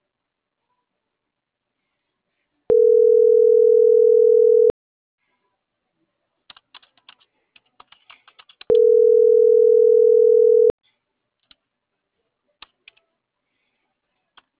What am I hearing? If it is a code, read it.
Ringback tone (North America), 440 + 480 Hz.